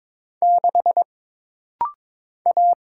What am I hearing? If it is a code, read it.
Morse code, heard as a keyed tone, about 22 wpm, 700 Hz. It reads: 6 A
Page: This rapid-fire timing is geared to push your ICR (Instant Character Recognition) skills.